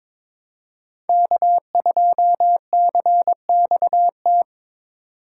Morse code, heard as a keyed tone, about 26 wpm, 700 Hz. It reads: K2CXT
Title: Speed Racing - USA 1x3 Call Signs 22wpm